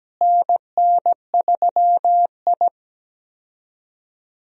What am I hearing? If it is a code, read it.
Morse code, heard as a keyed tone, about 17 wpm, 700 Hz. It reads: NN3I
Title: USA Call Signs - 4 Characters - Repeated 3x - 17wpm